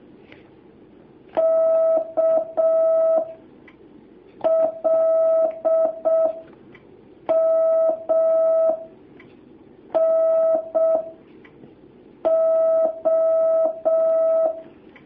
Morse code, heard as a keyed tone, about 6 wpm, spaced as for 4 wpm, 650 Hz. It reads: KLMNO
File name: jh8cbh / 2019cw